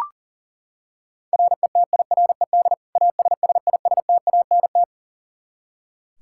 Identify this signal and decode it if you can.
Morse code, heard as a keyed tone, about 40 wpm, 700 Hz. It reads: RETIRED ASSISTANT
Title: K1USN SST QSO Practice 40wpm